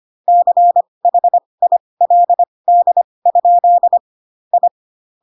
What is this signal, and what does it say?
Morse code, heard as a keyed tone, about 25 wpm, spaced as for 20 wpm, 700 Hz. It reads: CHILD? I
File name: Pinocchio / Pinocchio ch1 15 25 ik1hge